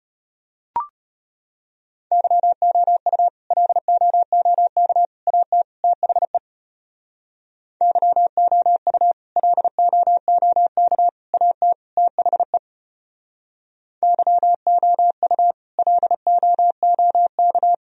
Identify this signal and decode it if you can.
Morse code, heard as a keyed tone, about 32 wpm, 700 Hz. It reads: YOU LOOK AT THE YOU LOOK AT THE YOU LOOK